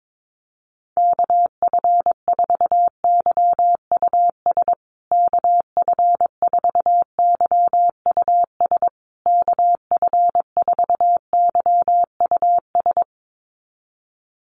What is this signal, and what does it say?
Morse code, heard as a keyed tone, about 22 wpm, 700 Hz. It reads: KF4YUS KF4YUS KF4YUS
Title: USA Call Signs Repeated 3x 22wpm